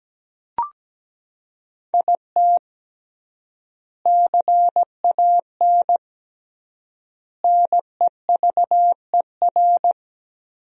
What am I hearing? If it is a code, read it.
Morse code, heard as a keyed tone, about 17 wpm, 700 Hz. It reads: IT CAN NEVER
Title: 3x Word-Spacing - Sentences from Top 400 Words - 17wpm